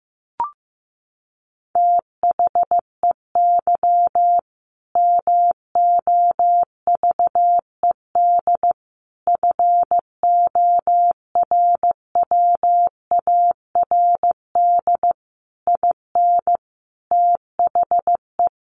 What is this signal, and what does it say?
Morse code, heard as a keyed tone, about 14 wpm, 700 Hz. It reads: THEY MOVED FORWARD IN THE
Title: A Sound Of Thunder 15wpm